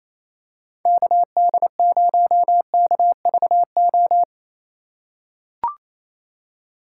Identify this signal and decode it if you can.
Morse code, heard as a keyed tone, about 28 wpm, 700 Hz. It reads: KD0KVO
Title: USA Call Signs - 3x Character Spacing - 28wpm